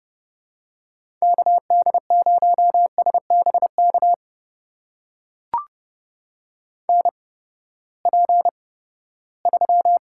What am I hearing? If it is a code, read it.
Morse code, heard as a keyed tone, about 31 wpm, spaced as for 17 wpm, 700 Hz. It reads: KD0SBK NP3
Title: USA Call Signs - 8x Character Spacing - 30wpm